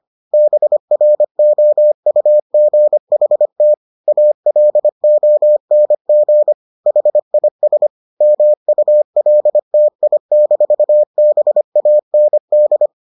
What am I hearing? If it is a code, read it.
Morse code, heard as a keyed tone, about 25 wpm, 600 Hz. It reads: BROUGHT ALONG HIS MULTI-BAND